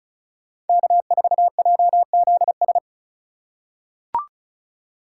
Morse code, heard as a keyed tone, about 36 wpm, 700 Hz. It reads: K4JZS
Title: USA Call Signs - 5 Characters - 3x Character Spacing - 35wpm